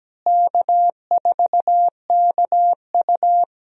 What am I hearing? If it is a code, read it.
Morse code, heard as a keyed tone, about 17 wpm, 700 Hz. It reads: K4KU
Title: USA Call Signs - 4 Characters - Repeated 3x - 17wpm